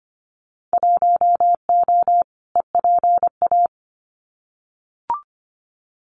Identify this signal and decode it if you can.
Morse code, heard as a keyed tone, about 25 wpm, 700 Hz. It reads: 1O EPA
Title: Winter Field Day Exchanges 25wpm